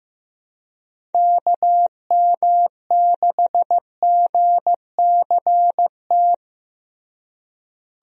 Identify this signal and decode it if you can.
Morse code, heard as a keyed tone, about 15 wpm, 700 Hz. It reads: KM6GCT
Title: USA Call Signs - 6 Characters - Repeated 2x - 15wpm